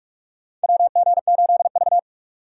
Morse code, heard as a keyed tone, about 46 wpm, 700 Hz. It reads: WG8V
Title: USA Call Signs - 2x Character Spacing - 45wpm